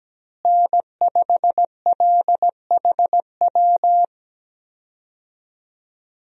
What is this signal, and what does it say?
Morse code, heard as a keyed tone, about 17 wpm, 700 Hz. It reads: N5LHW
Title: USA Call Signs - 5 Characters - Repeated 3x - 17wpm